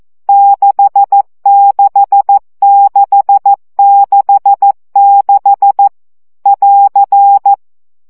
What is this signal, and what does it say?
Morse code, heard as a keyed tone, about 14 wpm, 800 Hz. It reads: 66666 <AR>
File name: CW gyakorlas 6 12 baud